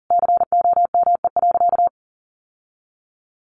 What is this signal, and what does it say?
Morse code, heard as a keyed tone, about 40 wpm, 700 Hz. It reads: COME.
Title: Sentences from Top 1500 Words 40wpm